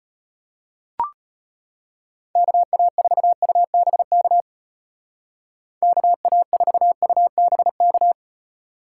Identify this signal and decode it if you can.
Morse code, heard as a keyed tone, about 34 wpm, 700 Hz. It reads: KA4UBK KA4UBK